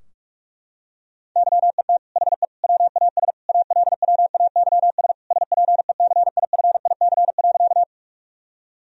Morse code, heard as a keyed tone, about 46 wpm, 700 Hz. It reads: YET HE WAS ALWAYS SPECIFIC.
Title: Sentences from Top 1000 Words - Review Part 3 - 45wpm